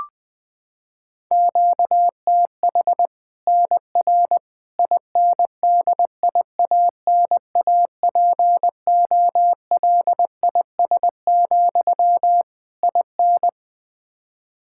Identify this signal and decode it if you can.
Morse code, heard as a keyed tone, about 21 wpm, 700 Hz. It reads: QTH NR INDIANAPOLIS, IN
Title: Practice QSO - Exchange 1 - Beginning - 20wpm